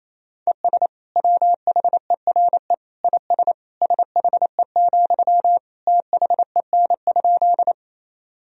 Morse code, heard as a keyed tone, about 28 wpm, 700 Hz. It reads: ES WHERE IS SHE, THEN?
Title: Sentences from Top 200 Words and Abbreviations 28wpm